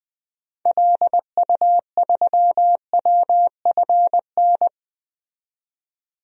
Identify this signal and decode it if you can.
Morse code, heard as a keyed tone, about 20 wpm, 700 Hz. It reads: LU3WFN
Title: DXCC-100 Call Signs - 6 Characters - 20wpm